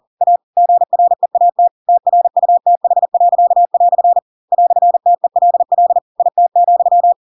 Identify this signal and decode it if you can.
Morse code, heard as a keyed tone, about 40 wpm, 700 Hz. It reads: A GREAT TRUTH." "TELL IT,
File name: Pinocchio / Pinocchio ch4 40 40 ik1hge